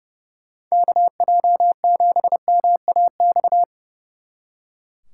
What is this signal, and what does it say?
Morse code, heard as a keyed tone, about 37 wpm, 700 Hz. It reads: KJ7MAX
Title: Speed Racing - USA 2x3 Call Signs 30wpm